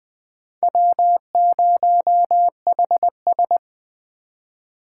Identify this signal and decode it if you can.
Morse code, heard as a keyed tone, about 21 wpm, 700 Hz. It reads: W0HS